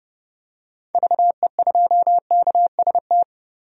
Morse code, heard as a keyed tone, about 30 wpm, 700 Hz. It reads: VE2KST
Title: DXCC-100 Call Signs - 6 Characters - 30wpm